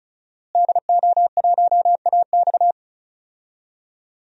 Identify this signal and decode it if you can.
Morse code, heard as a keyed tone, about 34 wpm, 700 Hz. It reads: DO1AX
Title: DXCC-100 Call Signs - 5 Characters - Repeated 3x - 35wpm